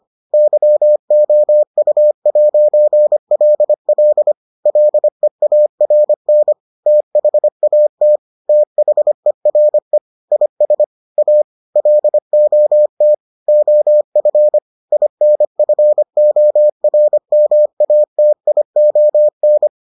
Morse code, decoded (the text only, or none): YOU'LL LEARN THAT THERE IS A LOT OF INFORMATION